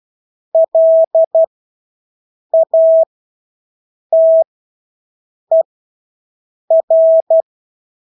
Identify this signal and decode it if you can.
Morse code, heard as a keyed tone, about 12 wpm, spaced as for 6 wpm, 650 Hz. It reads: LATER
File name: Engelske ord 12-6 wpm